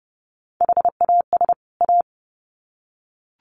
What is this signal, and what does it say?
Morse code, heard as a keyed tone, about 30 wpm, 700 Hz. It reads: HAS A
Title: Sets of 2 Words 30wpm